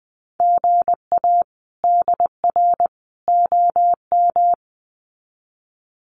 Morse code, heard as a keyed tone, about 20 wpm, 700 Hz. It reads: GA DR OM
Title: Common QSO Phrases 20wpm